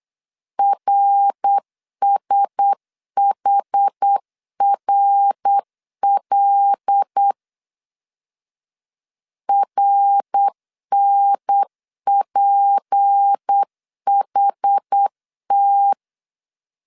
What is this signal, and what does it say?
Morse code, heard as a keyed tone, about 8 wpm, 800 Hz. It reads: RSHRL RNPHT
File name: LEZIONI / 6 PKX0 / 6 PKX 8wpm 3041